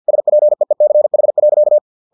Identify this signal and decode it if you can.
Morse code, heard as a keyed tone, about 50 wpm, 600 Hz. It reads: SPEECH.
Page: Never 49wpm